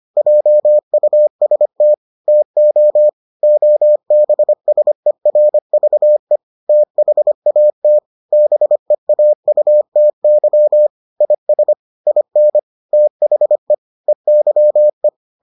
Morse code, heard as a keyed tone, about 25 wpm, 600 Hz. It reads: JUST TO OBSERVE THAT BEAUTY IS IN THE EYE